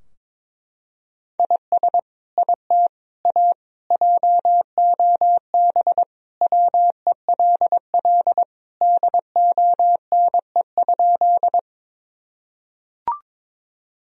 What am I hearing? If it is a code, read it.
Morse code, heard as a keyed tone, about 22 wpm, 700 Hz. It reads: IS IT A JOB WELL DONE?
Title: Sentences from Top 1000 Words - Review Part 3 - 22wpm